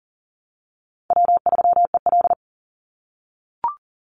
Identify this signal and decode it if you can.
Morse code, heard as a keyed tone, about 40 wpm, 700 Hz. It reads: W3EL
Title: Easy Call Signs - 40wpm